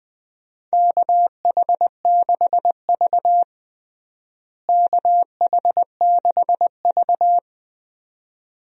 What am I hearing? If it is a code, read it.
Morse code, heard as a keyed tone, about 19 wpm, 700 Hz. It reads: KH6V KH6V
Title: DXCC-100 Call Signs - 4 Characters - Repeated 2x - 20wpm